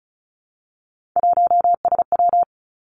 Morse code, heard as a keyed tone, about 36 wpm, 700 Hz. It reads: 1SW
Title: Three Challenging Letter-Number Combinations 35wpm